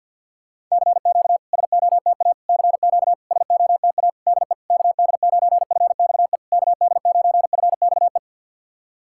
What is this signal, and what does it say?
Morse code, heard as a keyed tone, about 50 wpm, 700 Hz. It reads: CQ SOTA CQ SOTA DE KD9FXE KD9FXE
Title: SOTA QSO Practice 50wpm